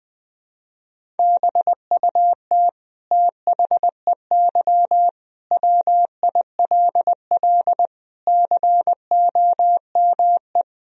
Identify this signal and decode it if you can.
Morse code, heard as a keyed tone, about 21 wpm, 700 Hz. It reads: BUT THEY WILL COME
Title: Sentences from Top 1000 Words - Review Part 1 - 20wpm